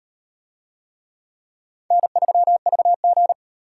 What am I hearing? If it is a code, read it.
Morse code, heard as a keyed tone, about 38 wpm, 700 Hz. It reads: N3VG